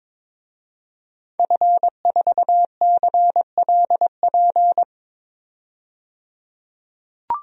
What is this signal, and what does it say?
Morse code, heard as a keyed tone, about 22 wpm, 700 Hz. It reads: F4CLP
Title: DXCC-100 Call Signs - 5 Characters - 22wpm